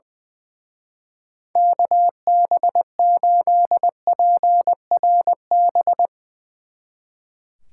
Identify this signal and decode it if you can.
Morse code, heard as a keyed tone, about 22 wpm, 700 Hz. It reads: KB8PRB
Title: Speed Racing - USA 2x3 Call Signs 20wpm